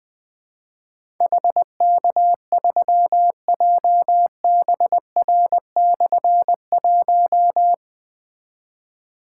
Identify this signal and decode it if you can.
Morse code, heard as a keyed tone, about 20 wpm, 700 Hz. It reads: HK3JBR/1
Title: Difficult Call Signs 20wpm